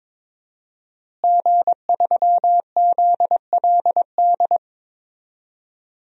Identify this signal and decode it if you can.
Morse code, heard as a keyed tone, about 22 wpm, 700 Hz. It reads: G3ZLD
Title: DXCC-100 Call Signs - 5 Characters - Repeated 3x - 22wpm